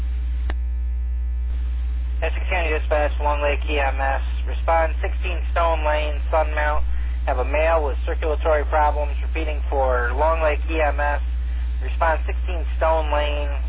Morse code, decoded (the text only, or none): none